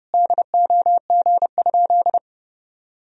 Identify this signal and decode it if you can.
Morse code, heard as a keyed tone, about 30 wpm, 700 Hz. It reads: DOG?
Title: Sentences from Top 1000 Words - Review Part 2 - 30wpm